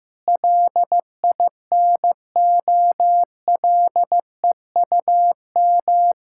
Morse code, heard as a keyed tone, about 14 wpm, 700 Hz. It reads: LINOLEUM
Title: Pangram Challenge v2 - 15wpm